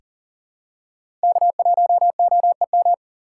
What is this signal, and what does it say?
Morse code, heard as a keyed tone, about 37 wpm, 690 Hz. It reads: K1OEM
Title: USA Call Signs - 8x Character Spacing - 40wpm